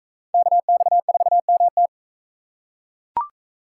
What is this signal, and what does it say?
Morse code, heard as a keyed tone, about 40 wpm, 700 Hz. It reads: KX4MT